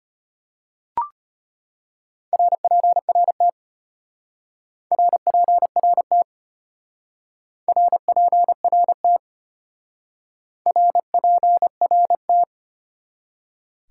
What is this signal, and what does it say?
Morse code, heard as a keyed tone, about 30 wpm, 700 Hz. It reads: RPRT RPRT RPRT RPRT
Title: Speed-Racing Top 90 QSO Elements 25wpm